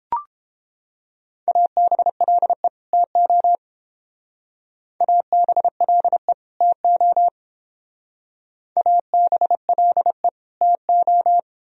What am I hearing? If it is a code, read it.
Morse code, heard as a keyed tone, about 30 wpm, 700 Hz. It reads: ABLE TO ABLE TO ABLE TO